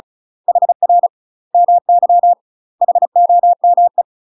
Morse code, heard as a keyed tone, about 34 wpm, 700 Hz. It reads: HR MY HOME
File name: UK MASTER QSOs TRAINING / 35WPM / UK MASTER QSOs 0042